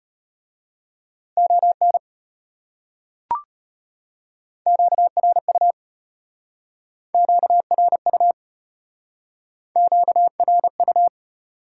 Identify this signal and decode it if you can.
Morse code, heard as a keyed tone, about 32 wpm, 700 Hz. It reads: ON QRU QRU QRU